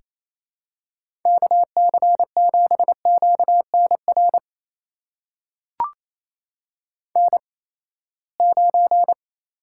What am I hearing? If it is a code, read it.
Morse code, heard as a keyed tone, about 28 wpm, spaced as for 19 wpm, 700 Hz. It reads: KC7QNR N9
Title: USA Call Signs - 8x Character Spacing - 28wpm